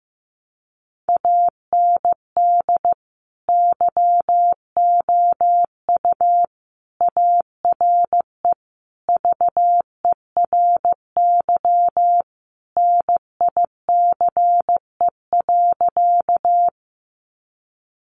Morse code, heard as a keyed tone, about 15 wpm, 700 Hz. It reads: AND YOU ARE VERY NICE.
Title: Sentences from Top 1100 Words 15wpm